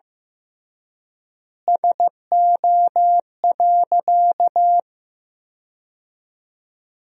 Morse code, heard as a keyed tone, about 15 wpm, 700 Hz. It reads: SO.
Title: 3x Word-Spacing - Sentences from Top 200 Words - 15wpm